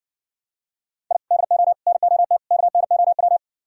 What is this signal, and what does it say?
Morse code, heard as a keyed tone, about 62 wpm, 700 Hz. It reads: I DO NOT KNOW